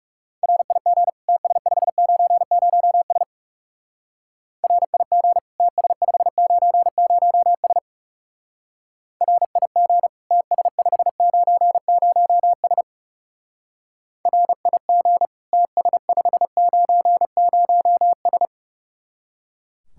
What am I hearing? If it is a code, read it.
Morse code, heard as a keyed tone, about 37 wpm, 700 Hz. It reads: RIG TS590S RIG TS590S RIG TS590S RIG TS590S
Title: Speed-Racing Common QSO Phrases - 30wpm